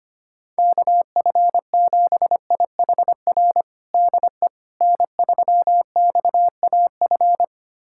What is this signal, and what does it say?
Morse code, heard as a keyed tone, about 25 wpm, 700 Hz. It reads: KF7IHR DE N3XAF
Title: USA Callsign DE Callsign 25wpm